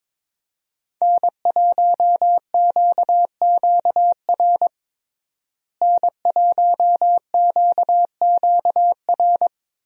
Morse code, heard as a keyed tone, about 22 wpm, 700 Hz. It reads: N1QQR N1QQR
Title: USA Call Signs - 5 Characters - Repeated 2x - 22wpm